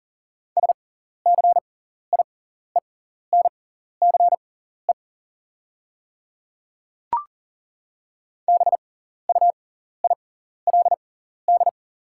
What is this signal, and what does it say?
Morse code, heard as a keyed tone, about 39 wpm, spaced as for 14 wpm, 690 Hz. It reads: SCIENCE BUILD
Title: 6x Character Spacing - Top 500 Words - 40wpm